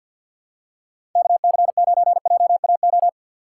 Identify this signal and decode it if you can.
Morse code, heard as a keyed tone, about 53 wpm, 700 Hz. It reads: KC9JAO
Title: USA Call Signs - 6 Characters - 4x Character Spacing - 50wpm